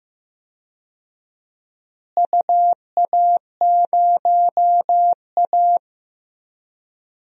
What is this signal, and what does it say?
Morse code, heard as a keyed tone, about 15 wpm, 700 Hz. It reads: UA0A
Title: DXCC-100 Call Signs - 4 Characters - Repeated 3x - 15wpm